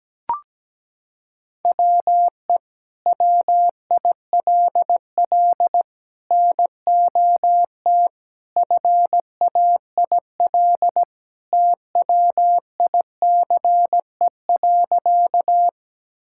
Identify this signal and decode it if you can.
Morse code, heard as a keyed tone, about 17 wpm, 700 Hz. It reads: WE WILL NOT FAIL TWICE.
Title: Sentences from Top 1600 Words 17wpm